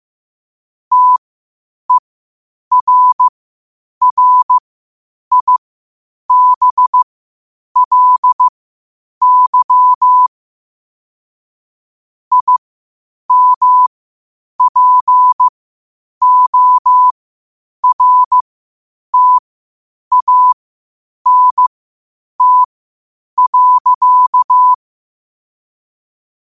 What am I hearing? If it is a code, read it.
Morse code, heard as a keyed tone, about 15 wpm, spaced as for 8 wpm, 1000 Hz. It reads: TERRIBLY IMPORTANT.
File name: qotdfiles / 8wpm